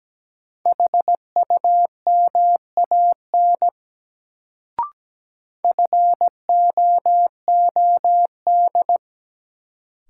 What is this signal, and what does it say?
Morse code, heard as a keyed tone, about 17 wpm, 700 Hz. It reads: HUMAN FOOD
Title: Sets of 5 Words - New Word Drill - 17wpm